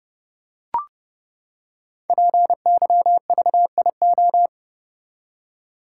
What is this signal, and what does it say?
Morse code, heard as a keyed tone, about 29 wpm, 700 Hz. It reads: PYVIO
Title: Code Groups - 5 Letter:Numbers - 30wpm